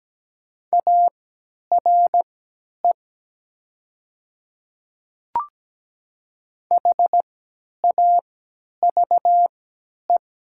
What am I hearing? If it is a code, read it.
Morse code, heard as a keyed tone, about 17 wpm, spaced as for 10 wpm, 700 Hz. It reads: ARE HAVE
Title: 3x Character Spacing - Top 100 Words - 17wpm